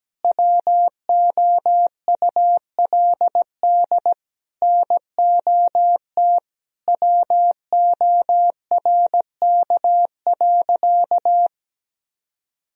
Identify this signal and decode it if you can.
Morse code, heard as a keyed tone, about 17 wpm, 700 Hz. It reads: WOULD NOT WORK.
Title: Sentences from Top 1000 Words - Review Part 3 - 17wpm